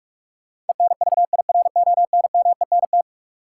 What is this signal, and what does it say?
Morse code, heard as a keyed tone, about 42 wpm, 700 Hz. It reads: ENVIRONMENT
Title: Sentences from Top 1000 Words - Review Part 5 - New Word Drill - 45wpm